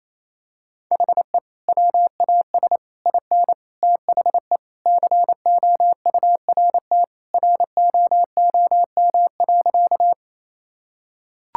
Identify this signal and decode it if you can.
Morse code, heard as a keyed tone, about 27 wpm, 700 Hz. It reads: HE WAS IN THE COURT ROOM.
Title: Sentences from Top 1600 Words 28wpm